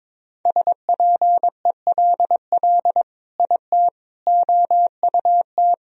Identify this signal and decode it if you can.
Morse code, heard as a keyed tone, about 22 wpm, 700 Hz. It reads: SPELL IT OUT
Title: Sentences from Top 1000 Words - Review Part 1 - 22wpm